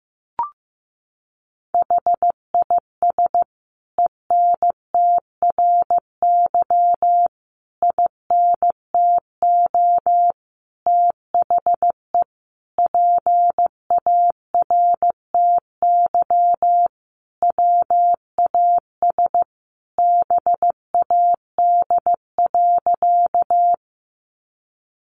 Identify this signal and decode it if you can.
Morse code, heard as a keyed tone, about 14 wpm, 700 Hz. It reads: HIS ENTRY INTO THE PARTY WAS BAD.
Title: Sentences from Top 1500 Words 15wpm